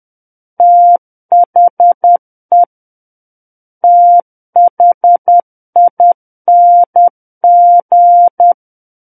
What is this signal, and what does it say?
Morse code, heard as a keyed tone, about 10 wpm, 700 Hz. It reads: THE THING
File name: Book One / 01-Chapter 1 - 10 WPM